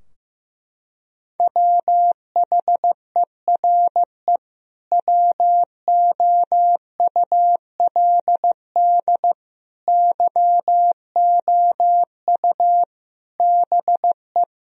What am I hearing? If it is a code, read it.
Morse code, heard as a keyed tone, about 15 wpm, 700 Hz. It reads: WHERE WOULD YOU BE
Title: Sentences from Top 1000 Words - Review Part 3 - 15wpm